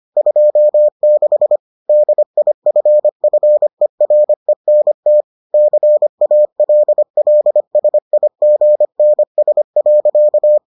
Morse code, heard as a keyed tone, about 25 wpm, 600 Hz. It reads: 26 DIFFERENT CALLSIGNS.